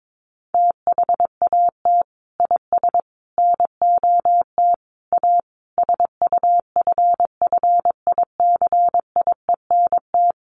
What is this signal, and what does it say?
Morse code, heard as a keyed tone, about 22 wpm, 700 Hz. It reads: THAT IS NOT A SUFFICIENT
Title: Sentences from Top 1400 Words 22wpm